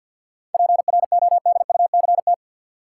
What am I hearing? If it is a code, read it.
Morse code, heard as a keyed tone, about 52 wpm, 700 Hz. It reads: PRODUCT
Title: Sentences from Top 1400 Words - New Word Drill Alt - 50wpm